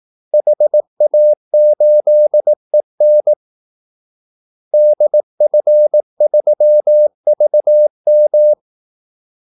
Morse code, heard as a keyed tone, about 18 wpm, 600 Hz. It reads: HA8EN DF3VM
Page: Kaldesignaler 18 wpm